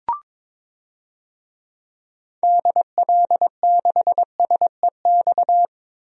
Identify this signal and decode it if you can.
Morse code, heard as a keyed tone, about 22 wpm, 700 Hz. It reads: DL6SEX
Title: DXCC-100 Call Signs - 6 Characters - Repeated 2x - 22wpm